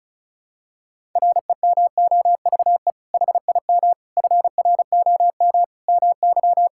Morse code, heard as a keyed tone, about 36 wpm, 700 Hz. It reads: REMOVE HIM FROM MY
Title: Sentences from Top 1000 Words - Review Part 5 - 35wpm